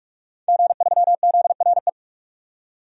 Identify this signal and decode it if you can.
Morse code, heard as a keyed tone, about 46 wpm, 700 Hz. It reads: G3ZRE
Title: DXCC-100 Call Signs - 5 Characters - 45wpm